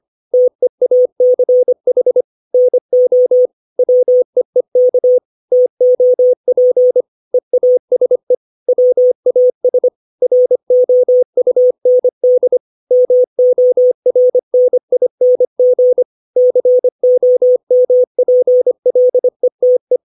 Morse code, heard as a keyed tone, about 25 wpm, 500 Hz. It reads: TEACH NO WEEK TOP EASE WAS ROUND MORNING COMPLETE
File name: 632 words / 632 Words 25WPM